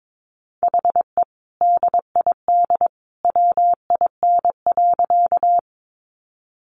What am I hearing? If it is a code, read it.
Morse code, heard as a keyed tone, about 22 wpm, 700 Hz. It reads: HE DID WIN.
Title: Sentences from Top 1500 Words 22wpm